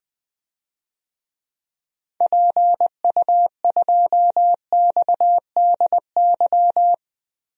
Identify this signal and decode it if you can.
Morse code, heard as a keyed tone, about 20 wpm, 700 Hz. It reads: PU2XDY
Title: DXCC-100 Call Signs - 6 Characters - Repeated 2x - 20wpm